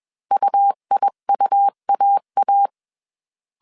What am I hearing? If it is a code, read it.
Morse code, heard as a keyed tone, about 21 wpm, 800 Hz. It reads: UIUAA
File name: CW / DET / 1 3 1500C S B 20wpm 1754